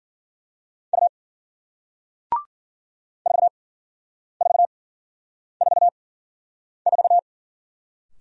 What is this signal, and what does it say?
Morse code, heard as a keyed tone, about 52 wpm, spaced as for 9 wpm, 700 Hz. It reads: U 4444